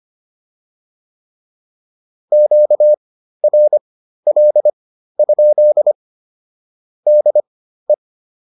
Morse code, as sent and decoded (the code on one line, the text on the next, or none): --.- .-. .-.. ..--.. / -.. .
QRL? DE